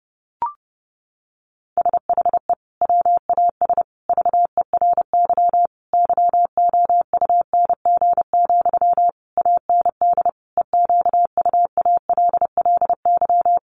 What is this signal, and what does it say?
Morse code, heard as a keyed tone, about 30 wpm, 700 Hz. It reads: SHE WAS VERY YOUNG, AND EQUALLY
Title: Sentences from Top 1400 Words 30wpm